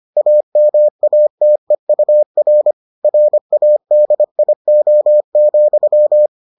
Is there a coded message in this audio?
Yes: Morse code from 0.2 to 6.3 s, heard as a keyed tone, about 25 wpm, 600 Hz. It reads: AMATEUR RADIO,